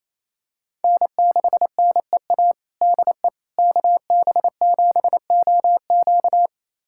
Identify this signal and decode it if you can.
Morse code, heard as a keyed tone, about 28 wpm, 700 Hz. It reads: N6NEA DE KB7OQ
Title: USA Callsign DE Callsign - 5 Characters - 28wpm